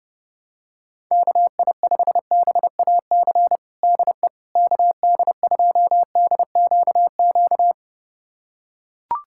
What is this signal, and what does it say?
Morse code, heard as a keyed tone, about 32 wpm, 700 Hz. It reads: KI5BAC DE KD2DQQ
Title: USA Callsign DE Callsign - 6 Characters - 30wpm